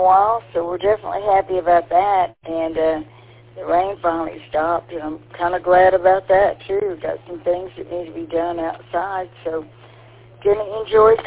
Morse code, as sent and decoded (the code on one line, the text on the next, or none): none